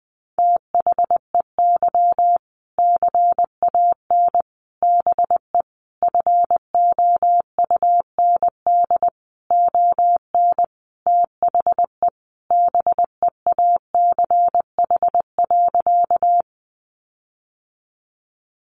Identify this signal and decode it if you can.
Morse code, heard as a keyed tone, about 20 wpm, 700 Hz. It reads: THEY CAN BE FOUND ON THE BEACH.
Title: Sentences from Top 1500 Words 20wpm